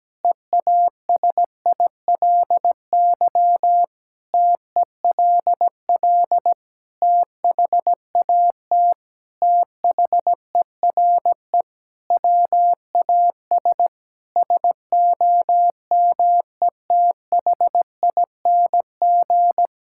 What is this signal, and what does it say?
Morse code, heard as a keyed tone, about 17 wpm, 700 Hz. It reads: EASILY TELL THAT THERE WAS SOMETHING
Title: Sentences from Top 1000 Words - Review Part 6 - 17wpm